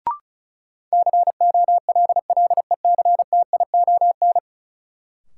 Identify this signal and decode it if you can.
Morse code, heard as a keyed tone, about 36 wpm, 700 Hz. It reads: COLLECTION
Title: Sentences from Top 1200 Words - New Word Drill - 35wpm